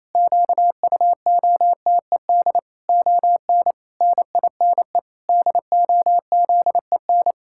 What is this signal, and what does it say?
Morse code, heard as a keyed tone, about 28 wpm, 700 Hz. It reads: QUOTED ON NINE DOZEN
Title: Pangram Challenge v2 - 28wpm